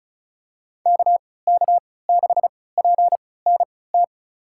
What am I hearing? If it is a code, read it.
Morse code, heard as a keyed tone, about 36 wpm, spaced as for 20 wpm, 700 Hz. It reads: KK6PNT
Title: USA Call Signs - 6 Characters - 3x Character Spacing - 35wpm